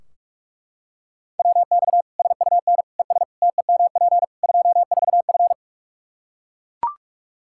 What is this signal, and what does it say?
Morse code, heard as a keyed tone, about 41 wpm, 700 Hz. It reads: WX SUN ES TEMP 24F
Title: Practice QSO - Exchange 2 - Info - 45wpm